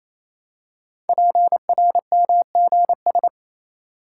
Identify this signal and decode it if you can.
Morse code, heard as a keyed tone, about 27 wpm, 700 Hz. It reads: PRMGS
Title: Code Groups - 5 Letters - 28wpm